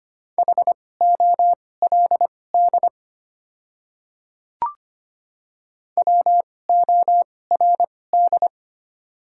Morse code, heard as a keyed tone, about 25 wpm, spaced as for 18 wpm, 700 Hz. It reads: HOLD WORD
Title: 2x Character Spacing - Top 500 Words - 25wpm